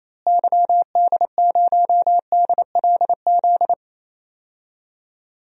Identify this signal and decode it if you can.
Morse code, heard as a keyed tone, about 28 wpm, 700 Hz. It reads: YD0DLZ
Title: DXCC-100 Call Signs - 6 Characters - Repeated 3x - 28wpm